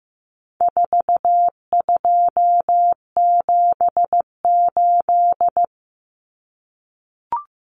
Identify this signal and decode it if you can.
Morse code, heard as a keyed tone, about 15 wpm, 700 Hz. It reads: 4278
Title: Numbers - 4-Digit 15wpm